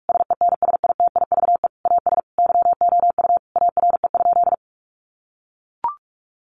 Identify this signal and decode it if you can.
Morse code, heard as a keyed tone, about 44 wpm, 700 Hz. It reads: SENSITIVE AS YOU ARE?
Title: Sentences from Top 1400 Words 45wpm